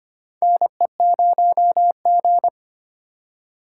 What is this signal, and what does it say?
Morse code, heard as a keyed tone, about 25 wpm, 700 Hz. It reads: NE0G
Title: USA Call Signs - 4 Characters - 3x Character Spacing - 25wpm